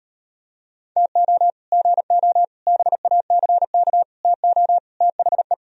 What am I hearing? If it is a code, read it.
Morse code, heard as a keyed tone, about 38 wpm, 700 Hz. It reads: TO GO BACK TO THE